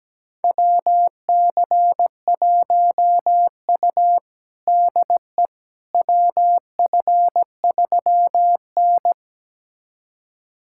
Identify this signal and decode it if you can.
Morse code, heard as a keyed tone, about 17 wpm, 700 Hz. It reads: WC1U DE WF3N
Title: USA Callsign DE Callsign - 4 Characters - 17wpm